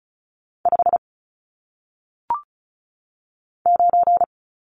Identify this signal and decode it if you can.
Morse code, heard as a keyed tone, about 36 wpm, 700 Hz. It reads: 5 9
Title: Numbers - 1-Digit 35wpm